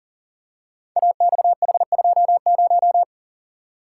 Morse code, heard as a keyed tone, about 39 wpm, 700 Hz. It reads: AXH20
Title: Code Groups - 5 Letter:Numbers - 40wpm